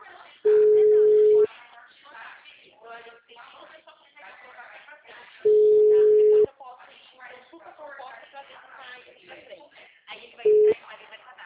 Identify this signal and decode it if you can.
Ringback tone (Europe, 425 Hz), with speech.